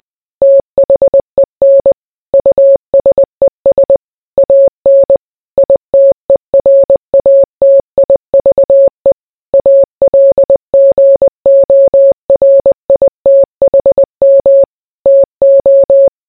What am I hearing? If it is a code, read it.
Morse code, heard as a keyed tone, about 20 wpm, 550 Hz. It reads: THEN USES AN ITERATIVE ALGORITHM TO